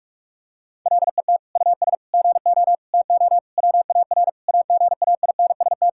Morse code, heard as a keyed tone, about 41 wpm, 700 Hz. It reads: LET US GO TO WAR AGAINST
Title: Sentences from Top 1000 Words - Review Part 5 - 45wpm